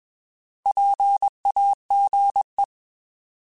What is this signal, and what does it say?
Morse code, heard as a keyed tone, about 21 wpm, 800 Hz. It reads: PAGE
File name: Alphabetical / Alphabetical mp3s / Ps @20